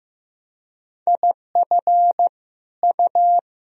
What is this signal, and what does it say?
Morse code, heard as a keyed tone, about 15 wpm, 700 Hz. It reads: IF U
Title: Sentences from Top 500 Words and Abbreviations 15wpm